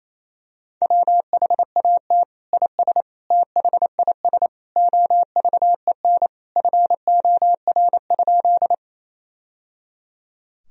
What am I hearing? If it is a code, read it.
Morse code, heard as a keyed tone, about 28 wpm, 700 Hz. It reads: WHAT IS THIS OVEN FOR?
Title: Sentences from Top 1000 Words - Review Part 4 - 28wpm